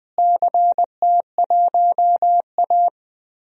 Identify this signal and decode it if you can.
Morse code, heard as a keyed tone, about 21 wpm, 700 Hz. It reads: CT1A
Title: DXCC-100 Call Signs - 4 Characters - Repeated 3x - 20wpm